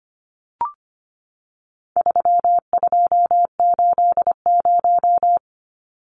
Morse code, heard as a keyed tone, about 25 wpm, 700 Hz. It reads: 3280
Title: Numbers - 4-Digit 25wpm